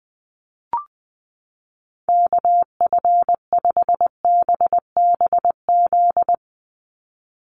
Morse code, heard as a keyed tone, about 20 wpm, 700 Hz. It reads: KF5BBZ
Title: USA Call Signs 20wpm